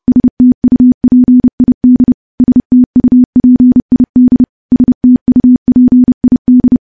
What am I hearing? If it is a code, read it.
Morse code, heard as a keyed tone, about 28 wpm, 250 Hz. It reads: STUPID STUPID STUPID
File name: morse / morse-training-30WPM